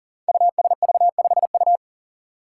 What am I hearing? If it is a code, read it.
Morse code, heard as a keyed tone, about 39 wpm, 700 Hz. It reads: USV5U
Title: Five Challenging Letter-Number Combinations 40wpm